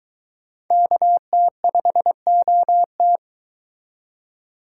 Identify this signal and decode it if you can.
Morse code, heard as a keyed tone, about 22 wpm, 700 Hz. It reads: KT5OT